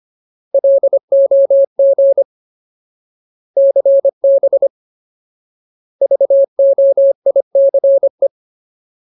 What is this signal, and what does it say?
Morse code, heard as a keyed tone, about 25 wpm, 550 Hz. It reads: LOG CB VOICE
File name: Ultimate Word List / 25WPM Ultimate